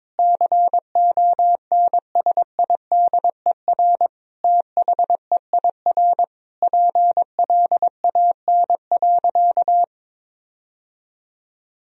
Morse code, heard as a keyed tone, about 22 wpm, 700 Hz. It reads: CONSIDER THEIR PLAN.
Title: Sentences from Top 1000 Words - Review Part 4 - 22wpm